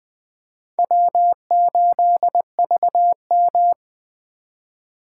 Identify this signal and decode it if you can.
Morse code, heard as a keyed tone, about 24 wpm, 700 Hz. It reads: W8VM
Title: Speed Racing - USA 1x2 Call Signs 20wpm